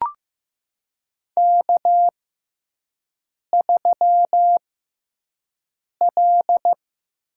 Morse code, heard as a keyed tone, about 15 wpm, spaced as for 5 wpm, 700 Hz. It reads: K3L
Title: USA Call Signs - 6x Character Spacing - 15wpm